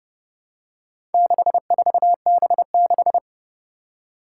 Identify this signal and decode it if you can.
Morse code, heard as a keyed tone, about 30 wpm, 700 Hz. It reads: 64B6
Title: Four Challenging Letter-Number Combinations 30wpm